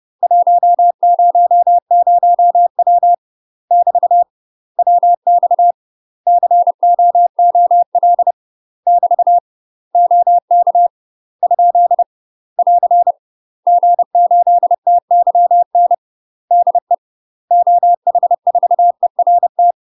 Morse code, heard as a keyed tone, about 30 wpm, 700 Hz. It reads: 100W <BT> WX COOL <BT> OK ? <AR> G8TYN DE OH4ERT
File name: UK MASTER QSOs TRAINING / 30WPM / UK MASTER QSOs 0008